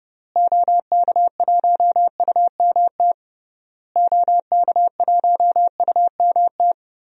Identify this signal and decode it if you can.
Morse code, heard as a keyed tone, about 30 wpm, 700 Hz. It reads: OK1UMT OK1UMT
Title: DXCC-100 Call Signs - 6 Characters - Repeated 3x - 30wpm